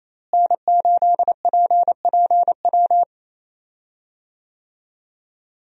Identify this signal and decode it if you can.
Morse code, heard as a keyed tone, about 28 wpm, 700 Hz. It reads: N8PPW
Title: USA Call Signs - 5 Characters - Repeated 3x - 28wpm